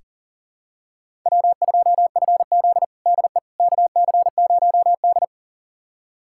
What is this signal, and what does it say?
Morse code, heard as a keyed tone, about 40 wpm, 700 Hz. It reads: W2FZ DE KC0D
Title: USA Callsign DE Callsign 40wpm